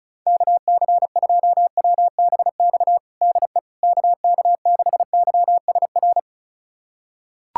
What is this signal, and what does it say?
Morse code, heard as a keyed tone, about 36 wpm, 700 Hz. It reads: KC2WBX DE KK6YSR
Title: USA Callsign DE Callsign - 6 Characters - 35wpm